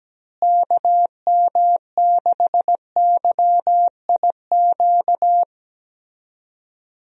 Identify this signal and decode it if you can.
Morse code, heard as a keyed tone, about 17 wpm, 700 Hz. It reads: KM6YIQ
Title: USA Call Signs - 6 Characters - Repeated 2x - 17wpm